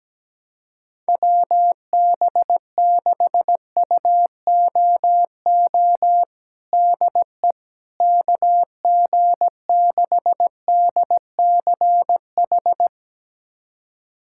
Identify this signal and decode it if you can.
Morse code, heard as a keyed tone, about 17 wpm, 700 Hz. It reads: WB6UOO DE KG6DCH
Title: USA Callsign DE Callsign 17wpm